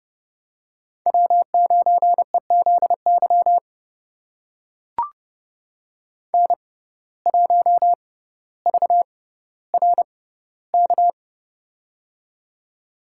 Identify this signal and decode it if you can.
Morse code, heard as a keyed tone, about 30 wpm, spaced as for 15 wpm, 700 Hz. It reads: W9EZY N1VRK
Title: USA Call Signs - 5 Characters - 6x Character Spacing - 30wpm